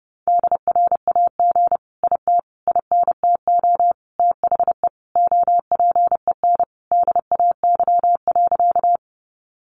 Morse code, heard as a keyed tone, about 30 wpm, 700 Hz. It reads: DRAG IT INTO THE OPEN DAY.
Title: Sentences from Top 1500 Words 30wpm